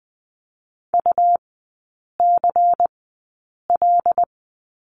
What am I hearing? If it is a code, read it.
Morse code, heard as a keyed tone, about 20 wpm, spaced as for 8 wpm, 700 Hz. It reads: UCL
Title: 114 Lesson 19e - Letter w - Three Characters 20wpm